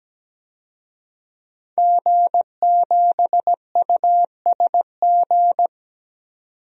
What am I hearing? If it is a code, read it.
Morse code, heard as a keyed tone, about 17 wpm, 700 Hz. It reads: G7USG
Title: DXCC-100 Call Signs - 5 Characters - 17wpm